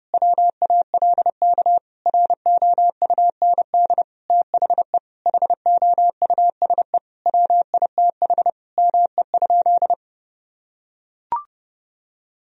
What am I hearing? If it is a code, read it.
Morse code, heard as a keyed tone, about 29 wpm, 700 Hz. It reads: WALK ROUND THE HOUSE WITH ME?
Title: Sentences from Top 1000 Words - Review Part 1 - 30wpm